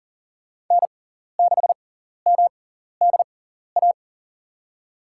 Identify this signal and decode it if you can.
Morse code, heard as a keyed tone, about 40 wpm, spaced as for 14 wpm, 700 Hz. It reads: N6MDA
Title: USA Call Signs - 6x Character Spacing - 40wpm